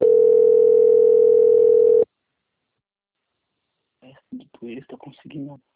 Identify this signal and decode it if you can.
Ringback tone (North America), 440 + 480 Hz, with speech.
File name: out- - - - - .56161